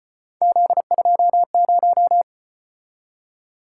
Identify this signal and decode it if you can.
Morse code, heard as a keyed tone, about 32 wpm, 700 Hz. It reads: Z20